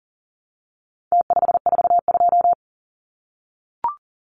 Morse code, heard as a keyed tone, about 40 wpm, 700 Hz. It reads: T542
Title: Numbers - Contest Style - 4-Digit 40wpm